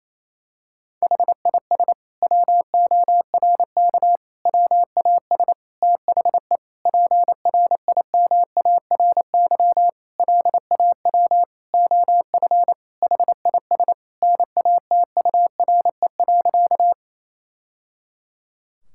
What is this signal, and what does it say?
Morse code, heard as a keyed tone, about 28 wpm, 700 Hz. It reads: HIS WORK WAS THE PRIMARY LAW OF HIS NATURE.
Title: Sentences from Top 1000 Words - Review Part 5 - 28wpm